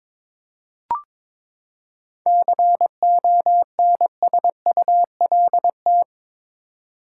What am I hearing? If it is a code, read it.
Morse code, heard as a keyed tone, about 22 wpm, 700 Hz. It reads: CONSULT
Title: Seven Letter Words - 22wpm